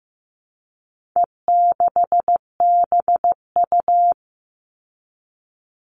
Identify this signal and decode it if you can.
Morse code, heard as a keyed tone, about 15 wpm, 700 Hz. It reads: E6BU
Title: All Cut Numbers - 4-Digit 15wpm